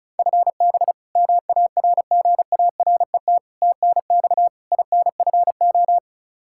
Morse code, heard as a keyed tone, about 36 wpm, 700 Hz. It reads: FB MARGARET TNX INFO
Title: Practice QSO - Exchange 3 - More Info - 35wpm